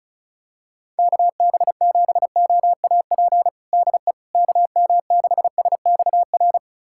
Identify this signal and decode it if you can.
Morse code, heard as a keyed tone, about 34 wpm, 700 Hz. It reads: KB7OAP DE KM6SXR
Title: USA Callsign DE Callsign - 6 Characters - 35wpm